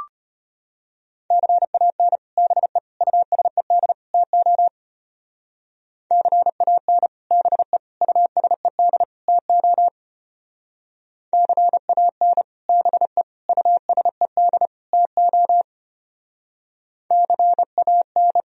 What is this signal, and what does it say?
Morse code, heard as a keyed tone, about 31 wpm, 700 Hz. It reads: CAN BE USED TO CAN BE USED TO CAN BE USED TO CAN
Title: Speed-Racing - Sets of 4 Words 25wpm